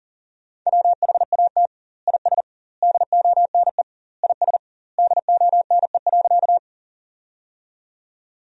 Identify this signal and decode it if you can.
Morse code, heard as a keyed tone, about 40 wpm, 700 Hz. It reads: WHAT IS DONE IS DONE.
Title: 2x Word-Spacing - Sentences from Top 500 Words - 40wpm